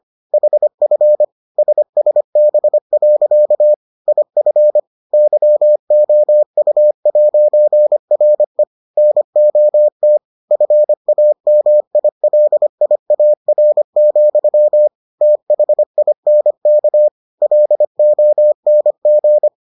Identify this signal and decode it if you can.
Morse code, heard as a keyed tone, about 25 wpm, 600 Hz. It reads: HF SSB. IF YOU'RE NOT FAMILIAR, THINK LONG